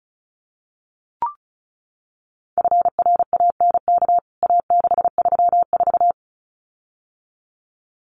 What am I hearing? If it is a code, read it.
Morse code, heard as a keyed tone, about 34 wpm, 700 Hz. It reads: FRANK A634
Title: CWT Exchange 35wpm